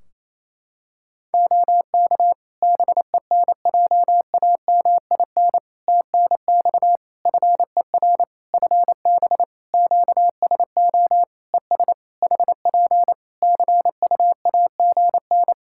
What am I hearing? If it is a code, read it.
Morse code, heard as a keyed tone, about 28 wpm, 700 Hz. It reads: OK BENJAMIN TNX FER FB QSO ES HP CUAGN
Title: Practice QSO - Exchange 4 - Ending - 28wpm